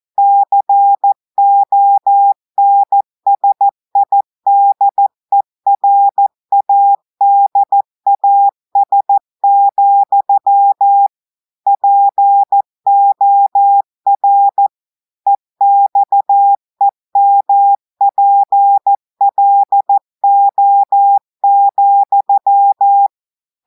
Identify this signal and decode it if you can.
Morse code, heard as a keyed tone, about 14 wpm, 800 Hz. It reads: CONSIDERADAS, POR EXEMPLO,
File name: audios / portugais / portugais texte en clair 2 vitesse 14 mots minutes